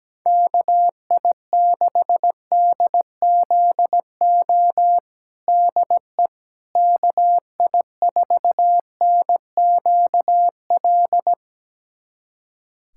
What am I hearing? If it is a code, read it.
Morse code, heard as a keyed tone, about 17 wpm, 700 Hz. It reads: KI6DZO DE KI4NQL
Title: USA Callsign DE Callsign 17wpm